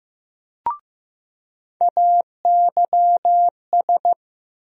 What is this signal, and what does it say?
Morse code, heard as a keyed tone, about 16 wpm, 700 Hz. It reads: AYS
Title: USA Call Sign Postfixes - 3 Letters 15wpm